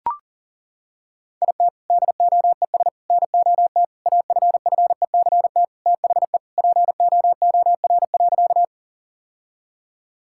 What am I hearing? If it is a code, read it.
Morse code, heard as a keyed tone, about 40 wpm, 700 Hz. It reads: IT DOES NOT AFFECT THE POOR.
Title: Sentences from Top 1000 Words - Review Part 6 - 40wpm